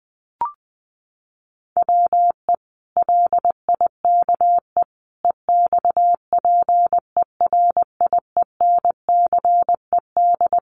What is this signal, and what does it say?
Morse code, heard as a keyed tone, about 20 wpm, 700 Hz. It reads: WE LIKE EXPERIENCED
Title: Sentences from Top 1100 Words 20wpm